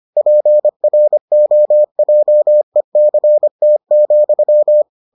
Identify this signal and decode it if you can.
Morse code, heard as a keyed tone, about 25 wpm, 600 Hz. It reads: PROJECT,